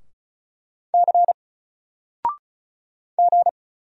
Morse code, heard as a keyed tone, about 36 wpm, 700 Hz. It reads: C G
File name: Single-Letter-Number-35wpm